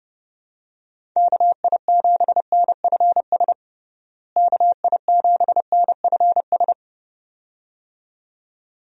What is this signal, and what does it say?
Morse code, heard as a keyed tone, about 30 wpm, 700 Hz. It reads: KI7NFS KI7NFS
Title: USA Call Signs - 6 Characters - Repeated 2x - 30wpm